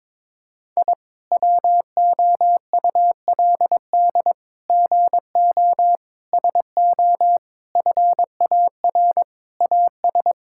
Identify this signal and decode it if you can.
Morse code, heard as a keyed tone, about 21 wpm, 700 Hz. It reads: I WOULD GO SO FAR AS
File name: practice / 22wpm / Sets of 8 Words - 22wpm